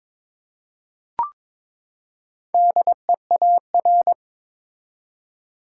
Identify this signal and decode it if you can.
Morse code, heard as a keyed tone, about 22 wpm, 700 Hz. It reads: DEAR
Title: Four Letter Words - 22wpm